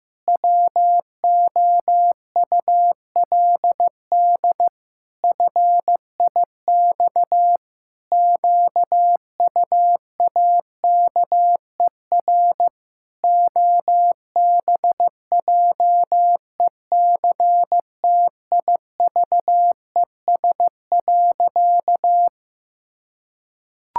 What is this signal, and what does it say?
Morse code, heard as a keyed tone, about 14 wpm, 700 Hz. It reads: WOULD FIX QUAKER OBJECTIVES.
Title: Pangram Challenge v2 - 15wpm